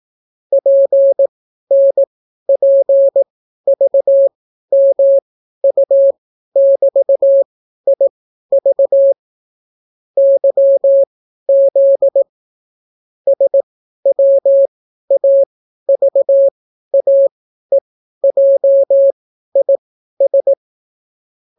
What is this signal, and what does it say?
Morse code, heard as a keyed tone, about 18 wpm, spaced as for 12 wpm, 550 Hz. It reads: PNPVMU<BT>IV YZ SWAVAEJIS